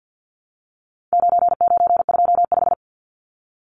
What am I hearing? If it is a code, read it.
Morse code, heard as a keyed tone, about 50 wpm, 700 Hz. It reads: 9925
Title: Numbers - 4-Digit 50wpm